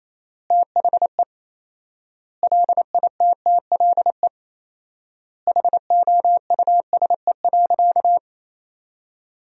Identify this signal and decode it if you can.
Morse code, heard as a keyed tone, about 27 wpm, 700 Hz. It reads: THE LITTLE HOUSE.
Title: 4x Word-Spacing - Sentences from Top 200 Words - 28wpm